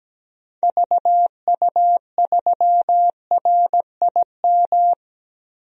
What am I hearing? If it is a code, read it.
Morse code, heard as a keyed tone, about 17 wpm, 700 Hz. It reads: VU3RIM
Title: DXCC-100 Call Signs - 6 Characters - 17wpm